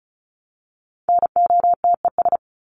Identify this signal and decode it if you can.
Morse code, heard as a keyed tone, about 34 wpm, 700 Hz. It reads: NOTES
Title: ES:S Suffix - Top 100 Words with Base Words 35wpm